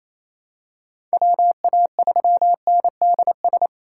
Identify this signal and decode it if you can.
Morse code, heard as a keyed tone, about 28 wpm, 700 Hz. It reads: WA3NDS
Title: USA Call Signs - 6 Characters - 2x Character Spacing - 28wpm